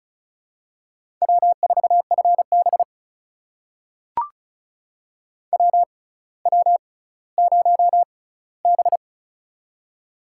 Morse code, heard as a keyed tone, about 36 wpm, spaced as for 18 wpm, 700 Hz. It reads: W4FB WW0B
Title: USA Call Signs - 4 Characters - 6x Character Spacing - 35wpm